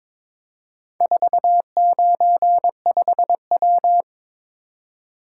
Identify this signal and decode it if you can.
Morse code, heard as a keyed tone, about 22 wpm, 700 Hz. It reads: 495W
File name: practice / 22wpm / Four Letter-Number Combinations - 22wpm